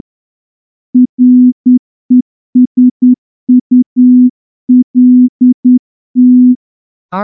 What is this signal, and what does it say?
Morse code, heard as a keyed tone, about 10 wpm, 250 Hz. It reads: RESULT
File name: morse-training-10WPM